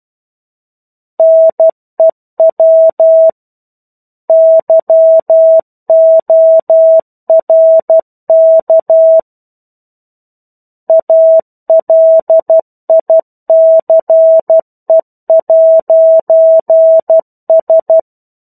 Morse code, heard as a keyed tone, about 12 wpm, 650 Hz. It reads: NEW YORK ALICE'S
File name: 01-Chapter-01 12 WPM